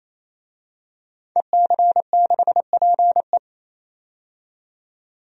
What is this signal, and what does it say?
Morse code, heard as a keyed tone, about 28 wpm, 700 Hz. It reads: EC6PE
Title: DXCC-100 Call Signs - 5 Characters - 28wpm